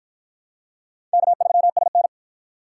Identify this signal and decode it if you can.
Morse code, heard as a keyed tone, about 52 wpm, 700 Hz. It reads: K3SN